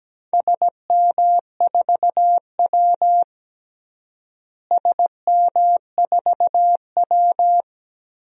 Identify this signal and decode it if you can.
Morse code, heard as a keyed tone, about 17 wpm, 700 Hz. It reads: SM4W SM4W
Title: DXCC-100 Call Signs - 4 Characters - Repeated 2x - 17wpm